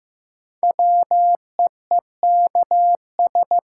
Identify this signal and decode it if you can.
Morse code, heard as a keyed tone, about 16 wpm, 700 Hz. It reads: WEEKS
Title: Sets of 5 Words - New Word Drill - 15wpm